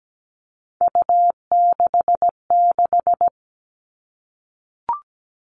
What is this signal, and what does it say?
Morse code, heard as a keyed tone, about 17 wpm, 700 Hz. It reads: U66
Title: Three Challenging Letter-Number Combinations 17wpm